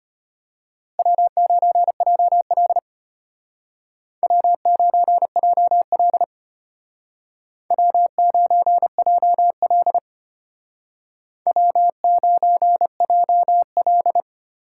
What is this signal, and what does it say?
Morse code, heard as a keyed tone, about 33 wpm, 700 Hz. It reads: W9JL W9JL W9JL W9JL